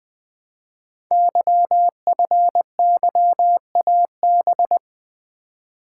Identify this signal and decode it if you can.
Morse code, heard as a keyed tone, about 20 wpm, 700 Hz. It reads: YFYAB
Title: Code Groups - 5 Letter:Numbers - 20wpm